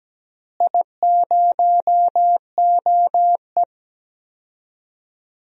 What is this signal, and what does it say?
Morse code, heard as a keyed tone, about 17 wpm, 700 Hz. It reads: I0OE
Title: DXCC-100 Call Signs - 4 Characters - Repeated 2x - 17wpm